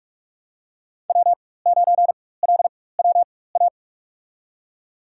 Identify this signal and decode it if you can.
Morse code, heard as a keyed tone, about 44 wpm, spaced as for 21 wpm, 700 Hz. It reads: W9LWA
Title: USA Call Signs - 5 Characters - 4x Character Spacing - 45wpm